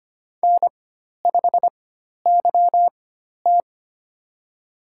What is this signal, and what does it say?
Morse code, heard as a keyed tone, about 25 wpm, spaced as for 12 wpm, 700 Hz. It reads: N5YT
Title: USA Call Signs - 4x Character Spacing - 25wpm